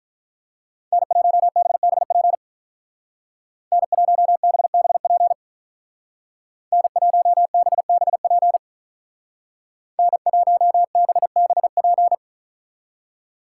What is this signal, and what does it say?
Morse code, heard as a keyed tone, about 45 wpm, 700 Hz. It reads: N1BBP N1BBP N1BBP N1BBP